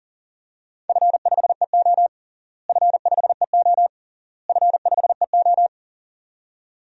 Morse code, heard as a keyed tone, about 37 wpm, 700 Hz. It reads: F5EO F5EO F5EO
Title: DXCC-100 Call Signs - 4 Characters - Repeated 3x - 40wpm